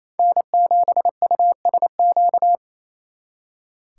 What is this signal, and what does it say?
Morse code, heard as a keyed tone, about 29 wpm, 700 Hz. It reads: N7USQ
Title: Speed Racing - USA 1x3 Call Signs 28wpm